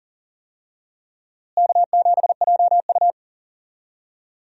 Morse code, heard as a keyed tone, about 40 wpm, 700 Hz. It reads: K7JU
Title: USA Call Signs - 4 Characters - Repeated 2x - 40wpm